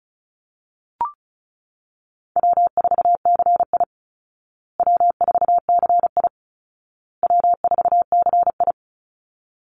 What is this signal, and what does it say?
Morse code, heard as a keyed tone, about 34 wpm, 700 Hz. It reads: W4CI W4CI W4CI
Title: Easy Call Signs Repeated 3x - 35wpm